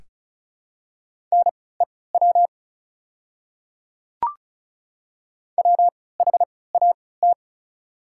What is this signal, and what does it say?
Morse code, heard as a keyed tone, about 34 wpm, spaced as for 20 wpm, 700 Hz. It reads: NEW WHAT
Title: 3x Character Spacing - Top 300 Words - 35wpm